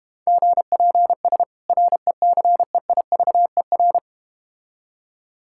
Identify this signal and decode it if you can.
Morse code, heard as a keyed tone, about 32 wpm, 700 Hz. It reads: GPS RECEIVER